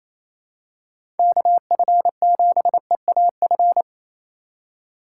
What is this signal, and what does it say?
Morse code, heard as a keyed tone, about 28 wpm, 700 Hz. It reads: KF7EAF
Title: USA Call Signs - 4x Character Spacing - 28wpm